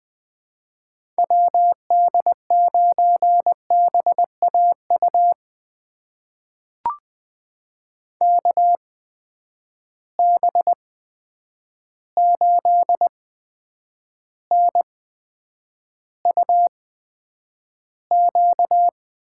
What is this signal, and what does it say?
Morse code, heard as a keyed tone, about 20 wpm, spaced as for 9 wpm, 700 Hz. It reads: WD9BAU KB8NUQ
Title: USA Call Signs - 6 Characters - 8x Character Spacing - 20wpm